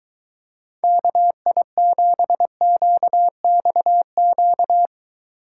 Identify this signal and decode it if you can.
Morse code, heard as a keyed tone, about 23 wpm, 700 Hz. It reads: KI7QXQ